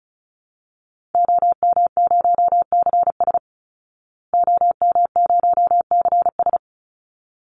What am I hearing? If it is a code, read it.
Morse code, heard as a keyed tone, about 34 wpm, 700 Hz. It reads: OM0CS OM0CS
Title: Call Signs Repeated 3x - 35wpm